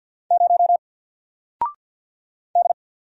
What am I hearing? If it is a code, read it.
Morse code, heard as a keyed tone, about 48 wpm, 700 Hz. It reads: 0 D
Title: Single Letter-Number Alt 50wpm